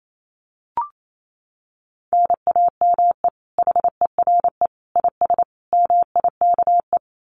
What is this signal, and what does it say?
Morse code, heard as a keyed tone, about 28 wpm, 700 Hz. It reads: NAME HERE IS MIKE
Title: Common QSO Phrases 28wpm